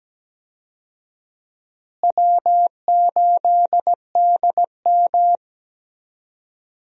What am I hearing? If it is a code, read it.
Morse code, heard as a keyed tone, about 17 wpm, 700 Hz. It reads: W8DM
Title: USA Call Signs - 4 Characters - Repeated 3x - 17wpm